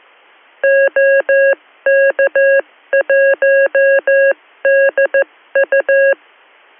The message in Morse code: OK1DU